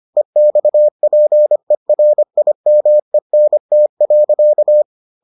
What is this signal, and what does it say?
Morse code, heard as a keyed tone, about 25 wpm, 600 Hz. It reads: EXPERIMENT.